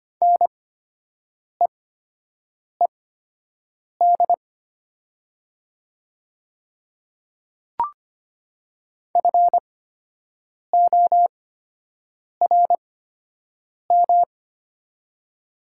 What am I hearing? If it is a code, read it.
Morse code, heard as a keyed tone, about 25 wpm, spaced as for 7 wpm, 700 Hz. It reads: NEED FORM
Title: 8x Character Spacing - Top 200 Words - 25wpm